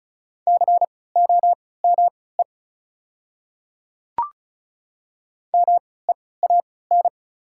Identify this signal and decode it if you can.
Morse code, heard as a keyed tone, about 36 wpm, spaced as for 20 wpm, 700 Hz. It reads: COME MEAN
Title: 3x Character Spacing - Top 200 Words - 35wpm